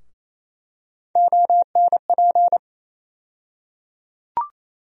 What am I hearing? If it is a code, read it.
Morse code, heard as a keyed tone, about 28 wpm, 700 Hz. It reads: ONP
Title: USA Call Sign Postfixes - 3 Letters 28wpm